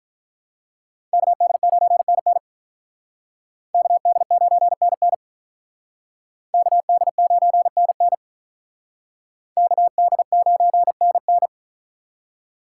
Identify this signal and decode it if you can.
Morse code, heard as a keyed tone, about 46 wpm, 700 Hz. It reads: KD9NN KD9NN KD9NN KD9NN